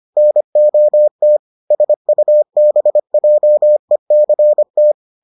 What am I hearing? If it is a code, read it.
Morse code, heard as a keyed tone, about 25 wpm, 600 Hz. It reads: NOT SUBJECT